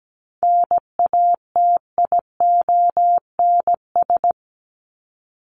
Morse code, heard as a keyed tone, about 17 wpm, 700 Hz. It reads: NATIONS
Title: ES:S Suffix - Top 100 Words with Base Words 17wpm